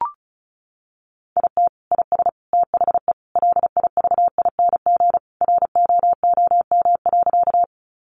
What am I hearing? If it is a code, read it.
Morse code, heard as a keyed tone, about 34 wpm, 700 Hz. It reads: IT IS THE LIVING ROOM.
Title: Sentences from Top 1100 Words 35wpm